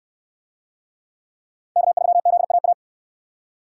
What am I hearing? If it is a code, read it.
Morse code, heard as a keyed tone, about 66 wpm, 700 Hz. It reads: K3ZAA